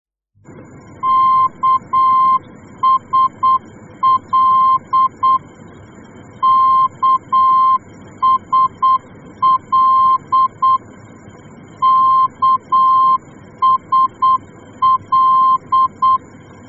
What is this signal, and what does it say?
Morse code, heard as a keyed tone, about 8 wpm, 1050 Hz. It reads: KSL KSL KSL